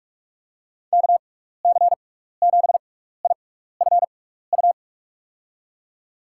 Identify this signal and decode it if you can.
Morse code, heard as a keyed tone, about 46 wpm, spaced as for 16 wpm, 700 Hz. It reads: KC7IFU
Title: USA Call Signs - 6 Characters - 6x Character Spacing - 45wpm